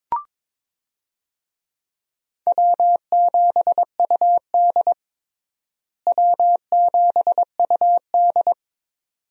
Morse code, heard as a keyed tone, about 21 wpm, 700 Hz. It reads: W7UD W7UD
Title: USA Call Signs - 4 Characters - Repeated 2x - 22wpm